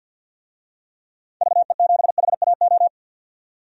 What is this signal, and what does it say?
Morse code, heard as a keyed tone, about 47 wpm, 700 Hz. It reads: VE7HAO
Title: DXCC-100 Call Signs - 6 Characters - 50wpm